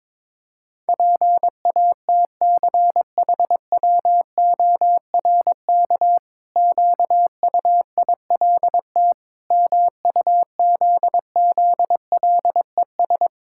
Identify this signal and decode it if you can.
Morse code, heard as a keyed tone, about 22 wpm, 700 Hz. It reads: PATCHWORK QUILT MUZZLES
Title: Pangram Challenge v2 - 22wpm